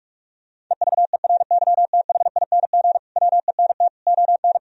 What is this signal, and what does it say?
Morse code, heard as a keyed tone, about 46 wpm, 700 Hz. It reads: EVERYTHING WENT ON
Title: Sentences from Top 1000 Words - Review Part 4 - 45wpm